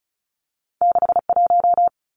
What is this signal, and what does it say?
Morse code, heard as a keyed tone, about 36 wpm, 700 Hz. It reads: 61
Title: Numbers - 2-Digit 35wpm